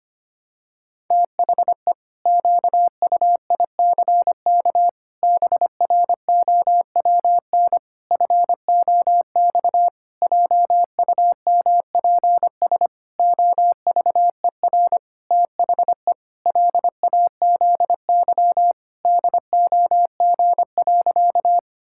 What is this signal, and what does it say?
Morse code, heard as a keyed tone, about 25 wpm, 700 Hz. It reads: THE QUICK BROWN FOX JUMPS OVER THE LAZY DOG.
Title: Pangram Challenge v2 - 25wpm